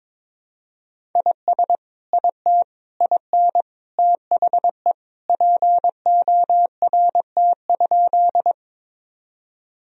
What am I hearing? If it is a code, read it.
Morse code, heard as a keyed tone, about 22 wpm, 700 Hz. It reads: IS IT IN THE PORT?
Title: Sentences from Top 1000 Words - Review Part 1 - 22wpm